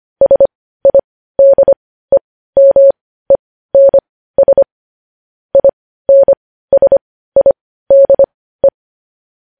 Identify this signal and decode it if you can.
Morse code, heard as a keyed tone, about 25 wpm, spaced as for 15 wpm, 550 Hz. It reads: SIDEMENS INSIDE
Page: News Headlines in Morse Code at 15 WPM